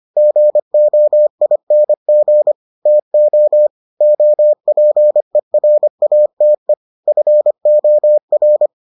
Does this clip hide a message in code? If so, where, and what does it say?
Yes: Morse code from 0.2 to 8.7 s, heard as a keyed tone, about 25 wpm, 600 Hz. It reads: GOING TO OPERATE FOR